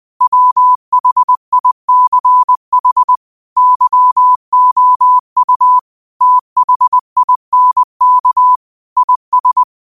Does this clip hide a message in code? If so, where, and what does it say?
Yes: Morse code from 0.2 to 9.6 s, heard as a keyed tone, about 20 wpm, 1000 Hz. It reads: WHICH YOU THINK IS